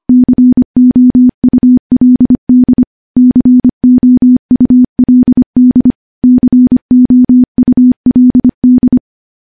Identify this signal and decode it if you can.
Morse code, heard as a keyed tone, about 25 wpm, 250 Hz. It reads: COULD COULD COULD
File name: morse / morse-training-25WPM